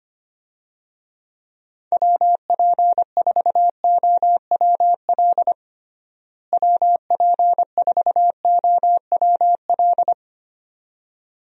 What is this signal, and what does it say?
Morse code, heard as a keyed tone, about 25 wpm, 700 Hz. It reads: WP4OWL WP4OWL
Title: USA Call Signs - 6 Characters - Repeated 2x - 25wpm